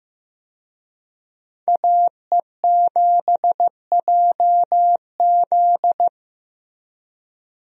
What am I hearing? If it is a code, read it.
Morse code, heard as a keyed tone, about 15 wpm, 700 Hz. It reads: AE7JZ
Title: USA Call Signs - 5 Characters - Repeated 3x - 15wpm